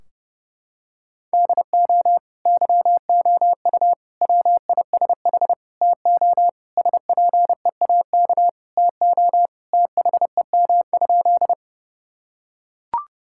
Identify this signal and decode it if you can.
Morse code, heard as a keyed tone, about 30 wpm, 700 Hz. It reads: DO YOU WISH TO SPEAK TO THEM?
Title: Sentences from Top 1600 Words 30wpm